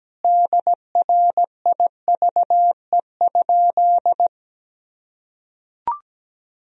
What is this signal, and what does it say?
Morse code, heard as a keyed tone, about 17 wpm, 700 Hz. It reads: DRIVE?
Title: Sentences from Top 1000 Words - Review Part 3 - 17wpm